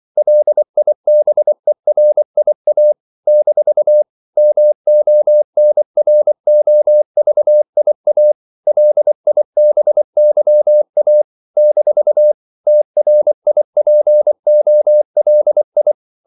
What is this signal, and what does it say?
Morse code, heard as a keyed tone, about 24 wpm, 600 Hz. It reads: LIBERIA - MONROVIA LIBYA - TRIPOLI
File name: CW / Practice / NationalCapitals24